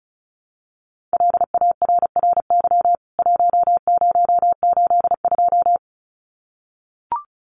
Morse code, heard as a keyed tone, about 34 wpm, 700 Hz. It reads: LARRY 1082
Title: CWT Exchange 35wpm